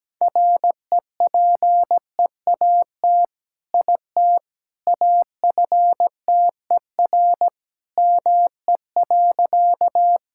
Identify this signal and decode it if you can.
Morse code, heard as a keyed tone, about 17 wpm, 700 Hz. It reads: REPEAT IT AFTER ME.
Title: Sentences from Top 1600 Words 17wpm